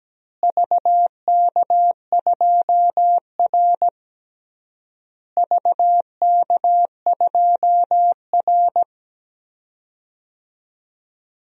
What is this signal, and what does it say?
Morse code, heard as a keyed tone, about 17 wpm, 700 Hz. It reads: VK2R VK2R
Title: DXCC-100 Call Signs - 4 Characters - Repeated 3x - 17wpm